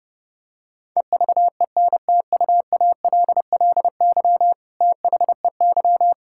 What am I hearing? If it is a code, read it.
Morse code, heard as a keyed tone, about 30 wpm, 700 Hz. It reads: EVENTUALLY THEY
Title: Sentences from Top 1000 Words - Review Part 6 - 30wpm